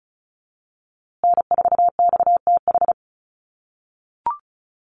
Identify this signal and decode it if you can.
Morse code, heard as a keyed tone, about 34 wpm, 700 Hz. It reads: N4XTH
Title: USA Call Signs Repeated 3x 35wpm